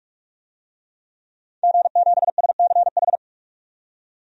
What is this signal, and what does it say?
Morse code, heard as a keyed tone, about 41 wpm, 700 Hz. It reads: G7SCH
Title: DXCC-100 Call Signs - 5 Characters - 45wpm